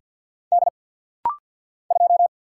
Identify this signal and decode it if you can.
Morse code, heard as a keyed tone, about 48 wpm, 690 Hz. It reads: D 2
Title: Single Letter-Number - Rapid-Fire 50wpm